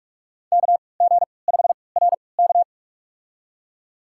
Morse code, heard as a keyed tone, about 46 wpm, spaced as for 26 wpm, 700 Hz. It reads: KG5RK
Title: USA Call Signs - 5 Characters - 3x Character Spacing - 45wpm